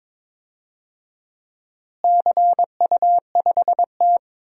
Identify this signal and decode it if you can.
Morse code, heard as a keyed tone, about 22 wpm, 700 Hz. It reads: CU5T
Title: DXCC-100 Call Signs - 4 Characters - Repeated 2x - 22wpm